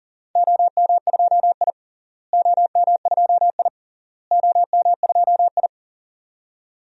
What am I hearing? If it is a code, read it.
Morse code, heard as a keyed tone, about 40 wpm, 700 Hz. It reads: OM2I OM2I OM2I
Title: DXCC-100 Call Signs - 4 Characters - Repeated 3x - 40wpm